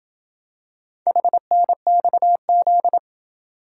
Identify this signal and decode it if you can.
Morse code, heard as a keyed tone, about 27 wpm, 700 Hz. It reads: HNXZ